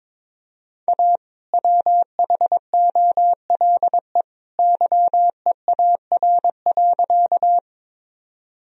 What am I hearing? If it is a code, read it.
Morse code, heard as a keyed tone, about 22 wpm, 700 Hz. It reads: A WHOLE YEAR.
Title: Sentences from Top 1000 Words - Review Part 1 - 22wpm